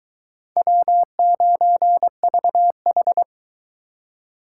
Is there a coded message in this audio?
Yes: Morse code from 0.6 to 3.2 s, heard as a keyed tone, about 24 wpm, 700 Hz. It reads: W9VH